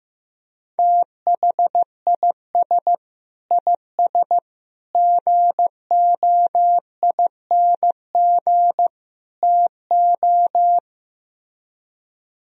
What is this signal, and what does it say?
Morse code, heard as a keyed tone, about 15 wpm, 700 Hz. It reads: THIS IS GOING TO
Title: Speed-Racing - Sets of 4 Words 15wpm